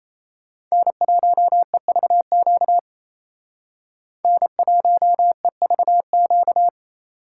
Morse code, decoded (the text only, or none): N1EVQ N1EVQ